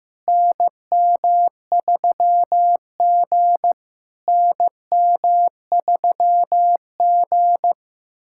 Morse code, heard as a keyed tone, about 15 wpm, 700 Hz. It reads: NM3G NM3G
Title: SOTA QSO Practice 15wpm